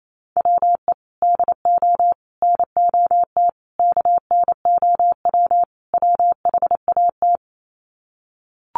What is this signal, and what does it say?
Morse code, heard as a keyed tone, about 28 wpm, 700 Hz. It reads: WE DO NOT KNOW WHAT
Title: Sets of 5 Words 28wpm